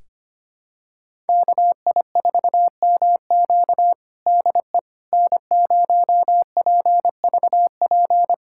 Morse code, heard as a keyed tone, about 25 wpm, 700 Hz. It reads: KI4MQ DE N0PVP
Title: USA Callsign DE Callsign - 5 Characters - 25wpm